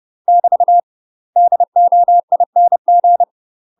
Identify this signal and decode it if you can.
Morse code, heard as a keyed tone, about 30 wpm, 700 Hz. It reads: <BT> DOING